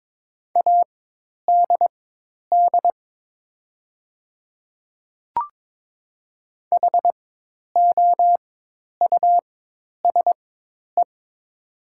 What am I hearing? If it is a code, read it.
Morse code, heard as a keyed tone, about 22 wpm, spaced as for 10 wpm, 700 Hz. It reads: ADD HOUSE
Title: 4x Character Spacing - Top 200 Words - 22wpm